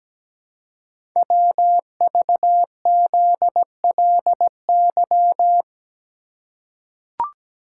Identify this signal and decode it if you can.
Morse code, heard as a keyed tone, about 17 wpm, 700 Hz. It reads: WVZLY
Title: Code Groups - 5 Letters - 17wpm